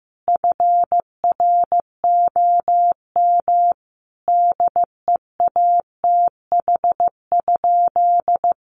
Morse code, heard as a keyed tone, about 15 wpm, 700 Hz. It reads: FROM DEATH?
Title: Sentences from Top 1100 Words 15wpm